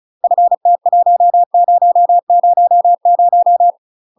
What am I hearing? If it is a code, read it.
Morse code, heard as a keyed tone, about 34 wpm, 700 Hz. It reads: FT1000
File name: UK MASTER QSOs TRAINING / 35WPM / UK MASTER QSOs 0025